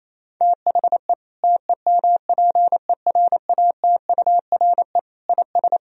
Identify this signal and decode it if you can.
Morse code, heard as a keyed tone, about 28 wpm, 700 Hz. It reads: THE TEMPERATURE IS
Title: Sentences from Top 1000 Words - Review Part 4 - 28wpm